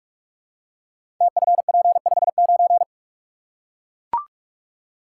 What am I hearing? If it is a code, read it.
Morse code, heard as a keyed tone, about 44 wpm, 690 Hz. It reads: TFP59
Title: Code Groups - 5 Letter:Numbers - 45wpm